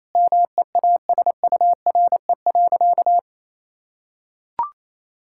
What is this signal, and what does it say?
Morse code, heard as a keyed tone, about 28 wpm, 700 Hz. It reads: MEASURE.
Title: Sentences from Top 1000 Words - Review Part 6 - 28wpm